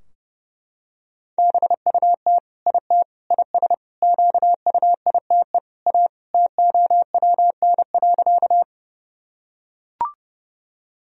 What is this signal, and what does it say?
Morse code, heard as a keyed tone, about 29 wpm, 700 Hz. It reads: BUT IT IS QUITE A TOWN.
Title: Sentences from Top 1000 Words - Review Part 2 - 30wpm